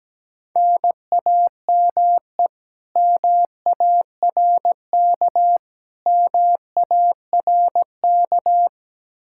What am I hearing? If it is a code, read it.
Morse code, heard as a keyed tone, about 17 wpm, 700 Hz. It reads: NAME MARK MARK
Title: Practice QSO - Exchange 1 - Beginning - 17wpm